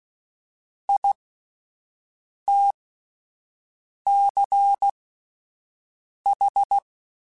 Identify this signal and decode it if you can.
Morse code, heard as a keyed tone, about 16 wpm, spaced as for 5 wpm, 800 Hz. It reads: ITCH
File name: Alphabetical / Alphabetical mp3s / Is @5